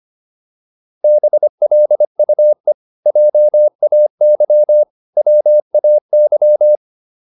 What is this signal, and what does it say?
Morse code, heard as a keyed tone, about 25 wpm, 600 Hz. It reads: BLUE JAY WAY